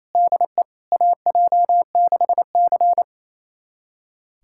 Morse code, heard as a keyed tone, about 28 wpm, 700 Hz. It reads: DE AJ6C
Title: USA Callsign DE Callsign - 4 Characters - 28wpm